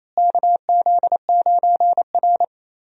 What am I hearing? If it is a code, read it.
Morse code, heard as a keyed tone, about 28 wpm, 700 Hz. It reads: KZ9R
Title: USA Call Signs - 4 Characters - Repeated 3x - 28wpm